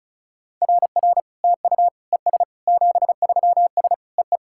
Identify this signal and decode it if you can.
Morse code, heard as a keyed tone, about 36 wpm, 700 Hz. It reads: RR TU ES 73S EE
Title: SOTA QSO Practice 35wpm